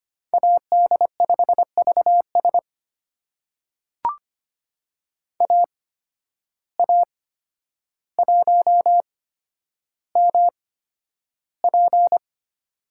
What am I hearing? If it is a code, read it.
Morse code, heard as a keyed tone, about 25 wpm, spaced as for 11 wpm, 700 Hz. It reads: AD5VS AA1MP
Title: USA Call Signs - 8x Character Spacing - 25wpm